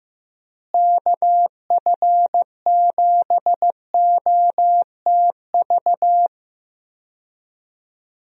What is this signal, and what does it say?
Morse code, heard as a keyed tone, about 15 wpm, 700 Hz. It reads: KF7OTV
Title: USA Call Signs - 6 Characters - Repeated 2x - 15wpm